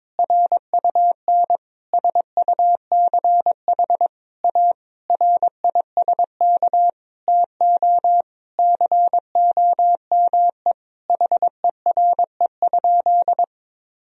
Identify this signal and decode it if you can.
Morse code, heard as a keyed tone, about 22 wpm, 700 Hz. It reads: RUN SUCH A RISK TO COME HERE?
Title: Sentences from Top 1000 Words - Review Part 4 - 22wpm